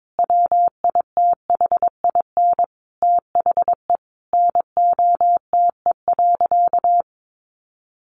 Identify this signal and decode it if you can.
Morse code, heard as a keyed tone, about 22 wpm, 700 Hz. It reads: WITHIN THE NOTE.
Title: Sentences from Top 1500 Words 22wpm